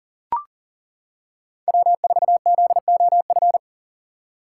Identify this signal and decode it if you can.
Morse code, heard as a keyed tone, about 40 wpm, 700 Hz. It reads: W4ZOF
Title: Code Groups - 5 Letter:Numbers - 40wpm